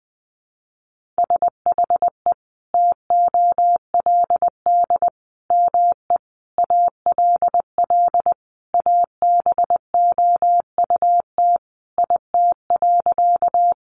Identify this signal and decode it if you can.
Morse code, heard as a keyed tone, about 19 wpm, 700 Hz. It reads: SHE TOLD ME ALL ABOUT IT.
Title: Sentences from Top 400 Words 20wpm